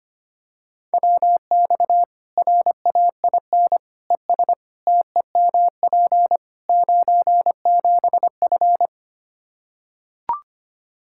Morse code, heard as a keyed tone, about 25 wpm, 700 Hz. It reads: WX RAIN ES TEMP 97F
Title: Practice QSO - Exchange 2 - Info - 25wpm